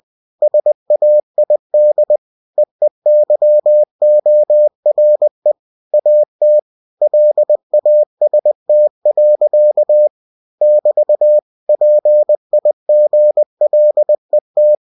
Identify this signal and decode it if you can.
Morse code, heard as a keyed tone, about 20 wpm, 600 Hz. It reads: SAID EEYORE AT LAST. <BT> PIGLET
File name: cw-books / winnie-the-pooh / winnie-20wpm-page-0056